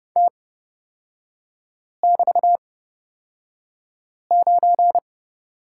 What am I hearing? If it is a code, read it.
Morse code, heard as a keyed tone, about 30 wpm, spaced as for 5 wpm, 700 Hz. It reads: T<BT>9
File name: Lesson 38e - Letter q - Three Characters 30wpm